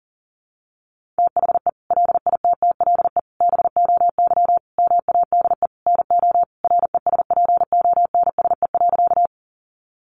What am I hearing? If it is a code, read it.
Morse code, heard as a keyed tone, about 37 wpm, 700 Hz. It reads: THE LITTLE BOY MADE NO RESPONSE.
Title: Sentences from Top 1400 Words 40wpm